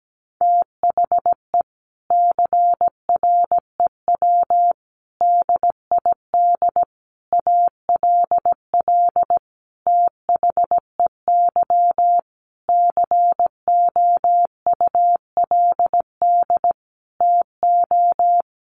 Morse code, heard as a keyed tone, about 17 wpm, 700 Hz. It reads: THE CREW DID ALL THEY COULD TO
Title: Sentences from Top 1500 Words 17wpm